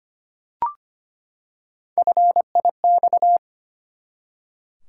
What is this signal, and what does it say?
Morse code, heard as a keyed tone, about 25 wpm, 700 Hz. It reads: FIX
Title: Three Letter Words - 25wpm